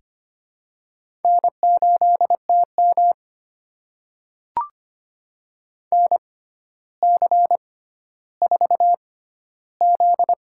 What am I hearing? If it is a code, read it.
Morse code, heard as a keyed tone, about 25 wpm, spaced as for 13 wpm, 700 Hz. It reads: N8TM NC4Z
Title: USA Call Signs - 4 Characters - 6x Character Spacing - 25wpm